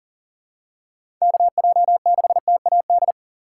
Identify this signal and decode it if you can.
Morse code, heard as a keyed tone, about 40 wpm, 700 Hz. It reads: KJ6TAD
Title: USA Call Signs - 6 Characters - 3x Character Spacing - 40wpm